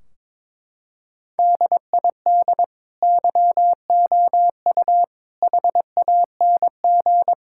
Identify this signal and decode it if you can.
Morse code, heard as a keyed tone, about 22 wpm, 700 Hz. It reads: DID YOU HANG
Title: Sentences from Top 1600 Words 22wpm